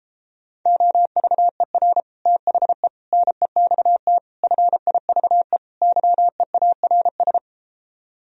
Morse code, heard as a keyed tone, about 33 wpm, 700 Hz. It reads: OVER THE NEXT FIVE YEARS